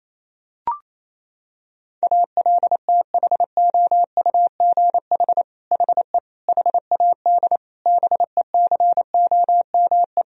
Morse code, heard as a keyed tone, about 27 wpm, 700 Hz. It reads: ALTHOUGH HE HAD BECOME
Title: Sentences from Top 1000 Words - Review Part 3 - 28wpm